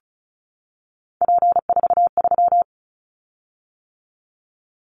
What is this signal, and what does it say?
Morse code, heard as a keyed tone, about 34 wpm, 700 Hz. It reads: P43
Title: Callsign Prefixes for Top 100 DXCC - 35wpm